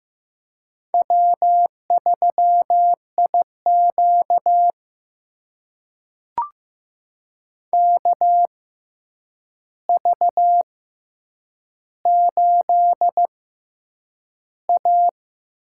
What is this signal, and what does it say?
Morse code, heard as a keyed tone, about 15 wpm, spaced as for 8 wpm, 700 Hz. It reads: W3IQ KV8A
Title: USA Call Signs - 4 Characters - 6x Character Spacing - 15wpm